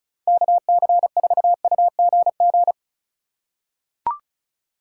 Morse code, heard as a keyed tone, about 34 wpm, 700 Hz. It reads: KC4UGG
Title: USA Call Signs - 6 Characters - 2x Character Spacing - 35wpm